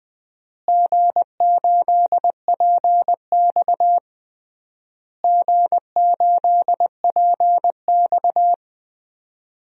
Morse code, heard as a keyed tone, about 20 wpm, 700 Hz. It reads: G8PX G8PX
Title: DXCC-100 Call Signs - 4 Characters - Repeated 3x - 20wpm